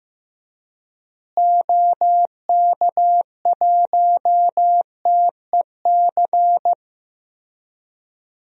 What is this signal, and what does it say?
Morse code, heard as a keyed tone, about 15 wpm, 700 Hz. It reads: OK1TEC
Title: DXCC-100 Call Signs - 6 Characters - Repeated 3x - 15wpm